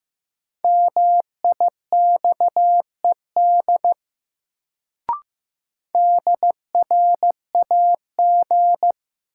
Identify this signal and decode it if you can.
Morse code, heard as a keyed tone, about 15 wpm, 700 Hz. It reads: MIXED DRAG
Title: Sentences from Top 1500 Words - New Word Drill - 15wpm